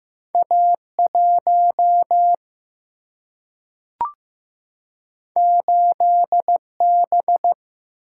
Morse code, heard as a keyed tone, about 15 wpm, 700 Hz. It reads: A 1 8 B
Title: Single-Letter Single-Number 15wpm